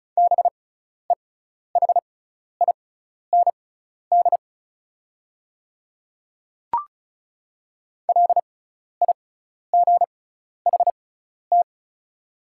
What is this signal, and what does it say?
Morse code, heard as a keyed tone, about 34 wpm, spaced as for 12 wpm, 700 Hz. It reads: BEHIND LIGHT
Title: 6x Character Spacing - Top 500 Words - 35wpm